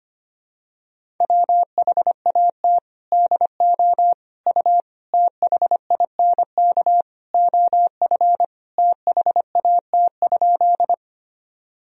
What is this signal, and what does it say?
Morse code, heard as a keyed tone, about 25 wpm, 700 Hz. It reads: WHAT DO U THINK OF THAT?
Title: Sentences from Top 200 Words and Abbreviations 25wpm